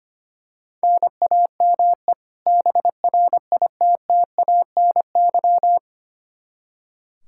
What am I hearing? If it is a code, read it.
Morse code, heard as a keyed tone, about 25 wpm, 700 Hz. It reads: NAME BRITTANY
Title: K1USN SST QSO Practice 25wpm